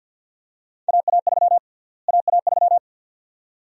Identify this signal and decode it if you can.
Morse code, heard as a keyed tone, about 48 wpm, 700 Hz. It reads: AA3 AA3
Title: USA Call Sign Prefixes - Repeated 2x - 50wpm